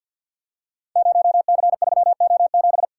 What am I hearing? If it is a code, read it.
Morse code, heard as a keyed tone, about 52 wpm, 700 Hz. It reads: 0C3O7
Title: Code Groups - 5 Letter:Numbers - 50wpm